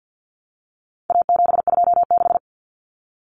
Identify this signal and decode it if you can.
Morse code, heard as a keyed tone, about 50 wpm, 700 Hz. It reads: A726
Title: Common Cut Numbers - 4-Digit 50wpm